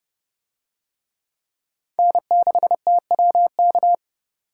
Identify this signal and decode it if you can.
Morse code, heard as a keyed tone, about 30 wpm, 700 Hz. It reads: N6TWK